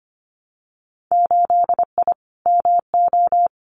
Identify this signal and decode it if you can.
Morse code, heard as a keyed tone, about 25 wpm, 700 Hz. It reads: 8I MO
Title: Winter Field Day Exchanges Encore 25wpm